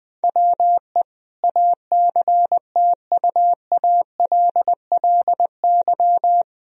Morse code, heard as a keyed tone, about 20 wpm, 700 Hz. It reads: WE ACTUALLY
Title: Sentences from Top 1000 Words - Review Part 4 - 20wpm